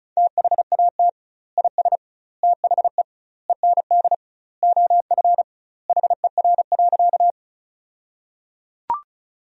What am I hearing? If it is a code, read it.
Morse code, heard as a keyed tone, about 36 wpm, 700 Hz. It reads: THAT IS THE END OF HER.
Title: 2x Word-Spacing - Sentences from Top 200 Words - 35wpm